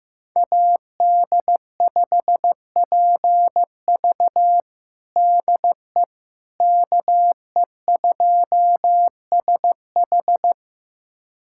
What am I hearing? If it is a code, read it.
Morse code, heard as a keyed tone, about 15 wpm, 700 Hz. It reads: AD5PV DE KE2SH
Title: USA Callsign DE Callsign - 5 Characters - 15wpm